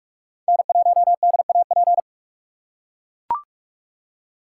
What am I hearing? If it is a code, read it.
Morse code, heard as a keyed tone, about 43 wpm, 690 Hz. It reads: N1DAP
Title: USA Call Signs - 5 Characters - 2x Character Spacing - 45wpm